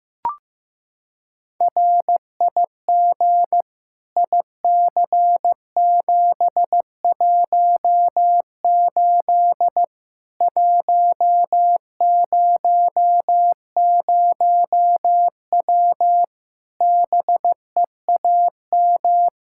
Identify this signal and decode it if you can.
Morse code, heard as a keyed tone, about 15 wpm, 700 Hz. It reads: RIG IC718 100W BEAM
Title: Practice QSO - Exchange 2 - Info - 15wpm